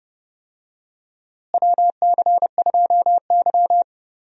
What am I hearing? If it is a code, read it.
Morse code, heard as a keyed tone, about 30 wpm, 700 Hz. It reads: WC2Y
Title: USA Call Signs - 4 Characters - Repeated 2x - 30wpm